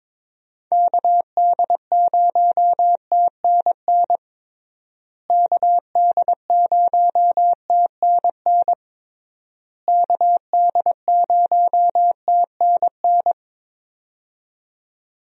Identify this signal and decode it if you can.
Morse code, heard as a keyed tone, about 21 wpm, 700 Hz. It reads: KD0TNN KD0TNN KD0TNN
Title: USA Call Signs - 6 Characters - Repeated 3x - 22wpm